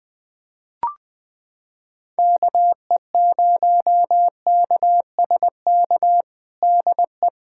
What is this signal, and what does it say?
Morse code, heard as a keyed tone, about 21 wpm, 700 Hz. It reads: KE0KSK DE
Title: USA Callsign DE Callsign 20wpm